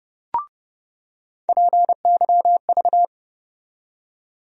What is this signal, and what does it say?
Morse code, heard as a keyed tone, about 30 wpm, 700 Hz. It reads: PYV
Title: USA Call Sign Postfixes - 3 Letters 30wpm